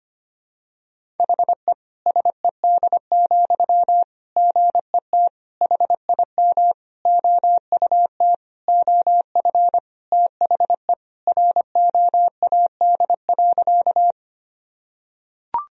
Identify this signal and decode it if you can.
Morse code, heard as a keyed tone, about 25 wpm, 700 Hz. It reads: HE SED, GET HIM OUT OF THE ROAD.
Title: Sentences from Top 400 Words and Abbreviations 25wpm